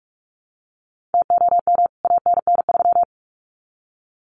Morse code, heard as a keyed tone, about 44 wpm, 700 Hz. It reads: TOM ANN3
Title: CWT Exchange 45wpm